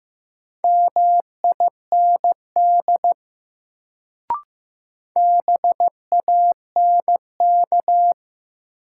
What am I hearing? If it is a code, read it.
Morse code, heard as a keyed tone, about 15 wpm, 700 Hz. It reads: MIND BANK
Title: Sets of 5 Words - New Word Drill - 15wpm